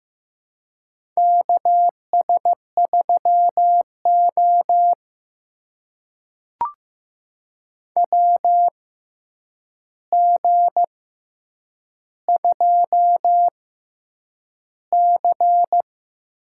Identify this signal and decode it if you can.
Morse code, heard as a keyed tone, about 15 wpm, spaced as for 8 wpm, 700 Hz. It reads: KS3O WG2C
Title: USA Call Signs - 4 Characters - 6x Character Spacing - 15wpm